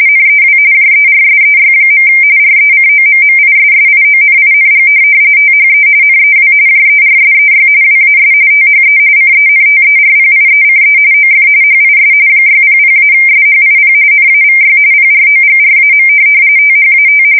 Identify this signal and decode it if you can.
RTTY, about 45.45 baud. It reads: NEW JERSEY. RECOMMISSIONING THE NEW JERSEY WAS THE CENTERPIECE OF HIS 600-SHIP NAVY PROGRAM THAT
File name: fsk170 / USS-New-Jersey